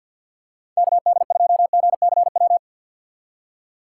Morse code, heard as a keyed tone, about 51 wpm, 690 Hz. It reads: KD2GCW
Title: USA Call Signs - 6 Characters - 3x Character Spacing - 50wpm